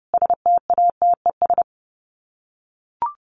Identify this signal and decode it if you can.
Morse code, heard as a keyed tone, about 30 wpm, 700 Hz. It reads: STATES
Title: Sets of 4 Words 30wpm